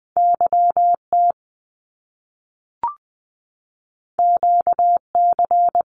Morse code, heard as a keyed tone, about 20 wpm, 700 Hz. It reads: YT QC
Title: Canadian Provinces and Territories 20wpm